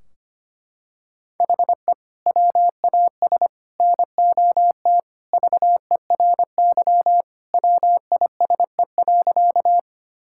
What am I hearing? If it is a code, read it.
Morse code, heard as a keyed tone, about 25 wpm, 700 Hz. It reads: HE WAS NOT VERY WISE.
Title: Sentences from Top 1600 Words 25wpm